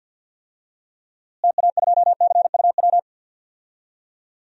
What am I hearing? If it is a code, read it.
Morse code, heard as a keyed tone, about 51 wpm, 700 Hz. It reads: TA2CUW
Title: DXCC-100 Call Signs - 6 Characters - Repeated 2x - 50wpm